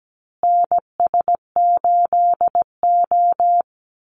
Morse code, heard as a keyed tone, about 17 wpm, 700 Hz. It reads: NS8O
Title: Call Signs Repeated 2x - 17wpm